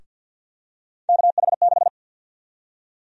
Morse code, heard as a keyed tone, about 52 wpm, 700 Hz. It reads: KH6
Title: USA Call Sign Prefixes - Repeated 2x - 50wpm